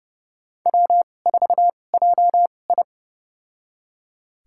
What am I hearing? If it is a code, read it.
Morse code, heard as a keyed tone, about 30 wpm, spaced as for 22 wpm, 700 Hz. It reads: W4JI
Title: USA Call Signs - 2x Character Spacing - 30wpm